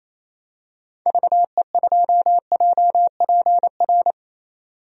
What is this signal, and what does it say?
Morse code, heard as a keyed tone, about 27 wpm, 700 Hz. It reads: VE2JPR
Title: DXCC-100 Call Signs - 6 Characters - 28wpm